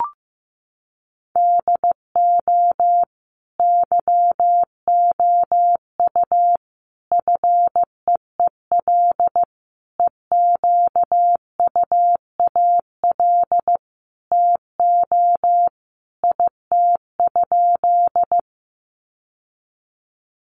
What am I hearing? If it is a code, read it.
Morse code, heard as a keyed tone, about 15 wpm, 700 Hz. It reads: DO YOU FEEL EQUAL TO IT?
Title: Sentences from Top 1400 Words 15wpm